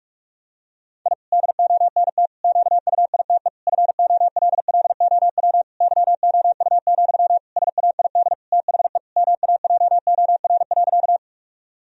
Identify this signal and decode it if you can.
Morse code, heard as a keyed tone, about 46 wpm, 700 Hz. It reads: I DONT QUITE FOLLOW YOU, SAID THE MAJOR.
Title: Sentences from Top 1000 Words - Review Part 4 - 45wpm